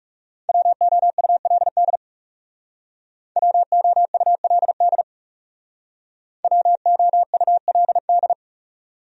Morse code, heard as a keyed tone, about 36 wpm, 700 Hz. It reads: WOULD WOULD WOULD
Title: Speed-Racing Top 500 Words 30wpm